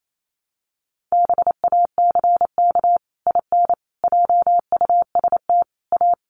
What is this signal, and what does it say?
Morse code, heard as a keyed tone, about 28 wpm, 700 Hz. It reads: BACK IN JUST A
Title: Sets of 5 Words Encore 28wpm